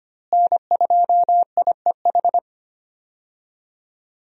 Morse code, heard as a keyed tone, about 25 wpm, 700 Hz. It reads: N2IEH
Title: USA Call Signs - 5 Characters - Repeated 3x - 25wpm